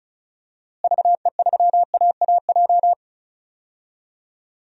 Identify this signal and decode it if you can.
Morse code, heard as a keyed tone, about 36 wpm, 700 Hz. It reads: VE3AAJ
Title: DXCC-100 Call Signs - 6 Characters - Repeated 3x - 35wpm